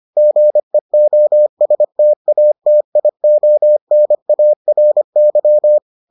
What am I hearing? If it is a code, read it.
Morse code, heard as a keyed tone, about 25 wpm, 600 Hz. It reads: GEOSTATIONARY